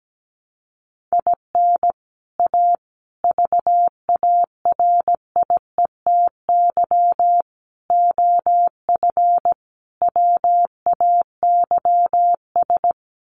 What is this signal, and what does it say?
Morse code, heard as a keyed tone, about 17 wpm, 700 Hz. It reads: IN A VARIETY OF WAYS
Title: Sets of 5 Words Encore 17wpm